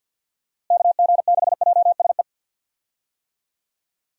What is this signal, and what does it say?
Morse code, heard as a keyed tone, about 53 wpm, 700 Hz. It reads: KG6JSE
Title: USA Call Signs - 6 Characters - 50wpm